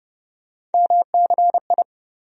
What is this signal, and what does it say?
Morse code, heard as a keyed tone, about 30 wpm, 700 Hz. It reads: MCI
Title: USA Call Sign Postfixes - 3 Letters 30wpm